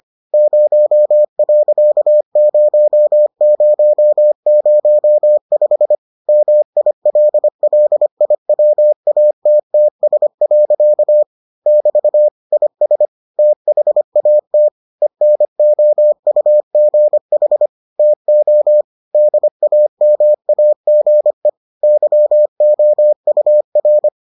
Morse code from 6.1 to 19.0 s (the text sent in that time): MILLIWATTS. <BT> IS THAT ENOUGH TO